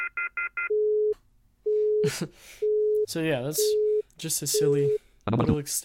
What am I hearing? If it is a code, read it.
Busy tone (Europe, 425 Hz), with speech.